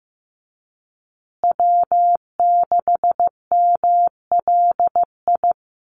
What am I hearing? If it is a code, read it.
Morse code, heard as a keyed tone, about 15 wpm, 700 Hz. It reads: W6MLI
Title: USA Call Signs Repeated 2x 15wpm